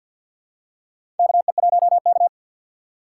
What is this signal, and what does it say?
Morse code, heard as a keyed tone, about 51 wpm, 700 Hz. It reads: KE1K
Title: USA Call Signs - 4 Characters - 2x Character Spacing - 50wpm